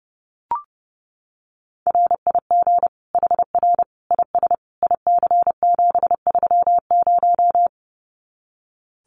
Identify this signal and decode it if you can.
Morse code, heard as a keyed tone, about 30 wpm, 700 Hz. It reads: RIG HR IS IC730
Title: Common QSO Phrases 30wpm